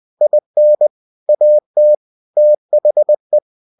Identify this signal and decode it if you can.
Morse code, heard as a keyed tone, about 20 wpm, 600 Hz. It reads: IN AT THE
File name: cw-books / winnie-the-pooh / winnie-20wpm-page-0037